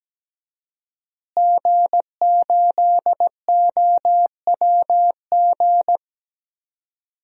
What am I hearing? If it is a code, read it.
Morse code, heard as a keyed tone, about 17 wpm, 700 Hz. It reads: G8OWG
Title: DXCC-100 Call Signs - 5 Characters - Repeated 3x - 17wpm